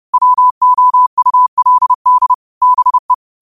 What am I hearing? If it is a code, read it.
Morse code, heard as a keyed tone, about 30 wpm, 1000 Hz. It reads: WOULD BE
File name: qotdfiles / 30wpm